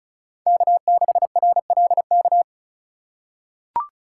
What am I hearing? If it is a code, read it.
Morse code, heard as a keyed tone, about 36 wpm, 700 Hz. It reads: K6RLK
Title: USA Call Signs - 5 Characters - 2x Character Spacing - 35wpm